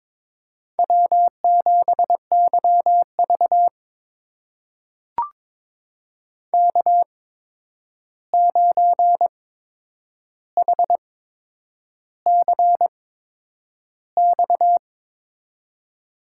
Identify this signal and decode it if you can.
Morse code, heard as a keyed tone, about 23 wpm, spaced as for 9 wpm, 700 Hz. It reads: W7YV K9HCX
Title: USA Call Signs - 8x Character Spacing - 22wpm